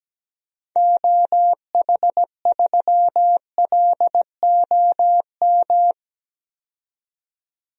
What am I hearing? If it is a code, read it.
Morse code, heard as a keyed tone, about 17 wpm, 700 Hz. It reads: OH3LOM
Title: DXCC-100 Call Signs - 6 Characters - 17wpm